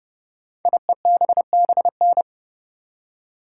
Morse code, heard as a keyed tone, about 30 wpm, 690 Hz. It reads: IEBBN
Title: Code Groups - 5 Letters - 30wpm